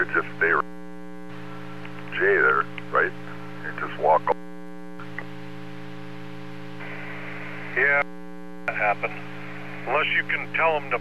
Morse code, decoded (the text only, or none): none